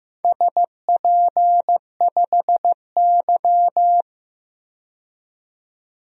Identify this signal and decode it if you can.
Morse code, heard as a keyed tone, about 15 wpm, 700 Hz. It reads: SP5Y
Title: DXCC-100 Call Signs - 4 Characters - 15wpm